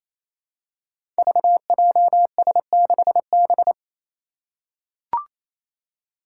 Morse code, heard as a keyed tone, about 27 wpm, 700 Hz. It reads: VJS6B
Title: Five Challenging Letter-Number Combinations 28wpm